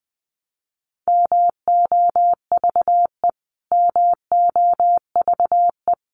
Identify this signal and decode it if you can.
Morse code, heard as a keyed tone, about 20 wpm, 700 Hz. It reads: MOVE MOVE
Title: Top 300 Words Repeated 20wpm